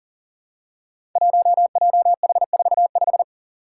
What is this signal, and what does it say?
Morse code, heard as a keyed tone, about 40 wpm, 700 Hz. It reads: 1JH45
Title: Five Challenging Letter-Number Combinations 40wpm